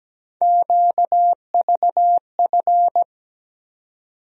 Morse code, heard as a keyed tone, about 17 wpm, 700 Hz. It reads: QVF
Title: USA Call Sign Postfixes - 3 Letters 17wpm